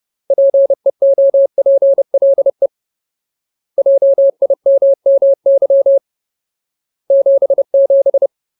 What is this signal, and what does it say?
Morse code, heard as a keyed tone, about 30 wpm, 550 Hz. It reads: PEOPLE JIMMY 77
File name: Ultimate Word List / 30WPM Ultimate